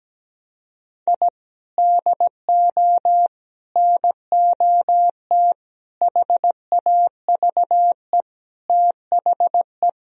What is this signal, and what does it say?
Morse code, heard as a keyed tone, about 17 wpm, 700 Hz. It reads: I DO NOT HAVE THE
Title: Sentences from Top 1000 Words - Review Part 3 - 17wpm